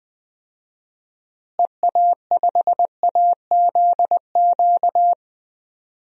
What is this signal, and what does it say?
Morse code, heard as a keyed tone, about 20 wpm, 700 Hz. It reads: EA5AZQ
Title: DXCC-100 Call Signs - 6 Characters - Repeated 3x - 20wpm